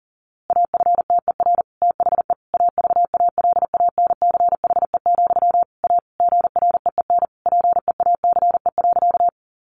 Morse code, heard as a keyed tone, about 40 wpm, 700 Hz. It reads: AFTER THE AVALANCHE, A GREEN PEACE.
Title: A Sound Of Thunder 40wpm